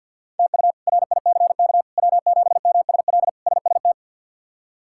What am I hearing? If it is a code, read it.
Morse code, heard as a keyed tone, about 51 wpm, 700 Hz. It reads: TU RICK W7MSL SST
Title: K1USN SST QSO Practice - 50wpm